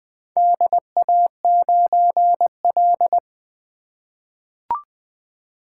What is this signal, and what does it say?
Morse code, heard as a keyed tone, about 20 wpm, 700 Hz. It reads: DA9L
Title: Four Letter-Number Combinations - 20wpm